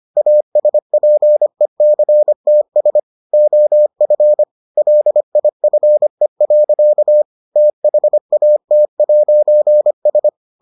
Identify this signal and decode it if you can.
Morse code, heard as a keyed tone, about 25 wpm, 600 Hz. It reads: ASPECTS OF LIFE. THAT'S